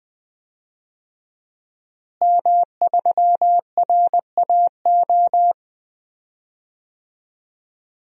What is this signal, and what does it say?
Morse code, heard as a keyed tone, about 20 wpm, 700 Hz. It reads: M3RAO
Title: DXCC-100 Call Signs - 5 Characters - Repeated 3x - 20wpm